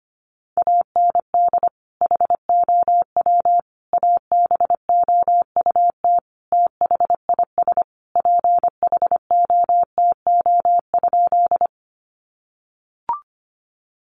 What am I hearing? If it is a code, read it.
Morse code, heard as a keyed tone, about 25 wpm, 700 Hz. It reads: AND HOW ABOUT THIS PHOTO?
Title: Sentences from Top 1200 Words 25wpm